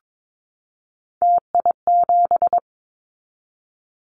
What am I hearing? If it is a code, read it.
Morse code, heard as a keyed tone, about 22 wpm, 700 Hz. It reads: TI7
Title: Callsign Prefixes for Top 100 DXCC - 22wpm